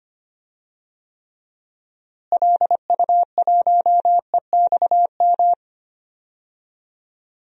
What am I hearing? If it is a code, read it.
Morse code, heard as a keyed tone, about 25 wpm, 700 Hz. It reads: LU1EXM
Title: DXCC-100 Call Signs - 6 Characters - Repeated 3x - 25wpm